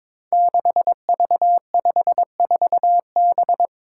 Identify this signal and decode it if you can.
Morse code, heard as a keyed tone, about 22 wpm, 700 Hz. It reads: 6V54B
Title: Five Challenging Letter-Number Combinations 22wpm